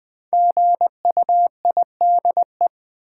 Morse code, heard as a keyed tone, about 20 wpm, 700 Hz. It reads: GUIDE
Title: Sentences from Top 600 Words - New Word Drill - 20wpm